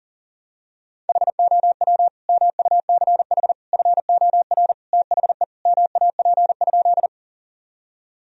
Morse code, heard as a keyed tone, about 40 wpm, 700 Hz. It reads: HOW MUCH FOR THE MAP?
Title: Sentences from Top 1000 Words - Review Part 2 - 40wpm